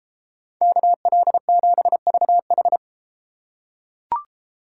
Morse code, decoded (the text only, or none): KL7VH